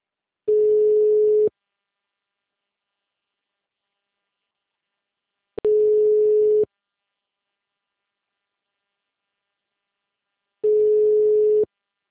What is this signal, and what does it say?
Ringback tone (Europe, 425 Hz).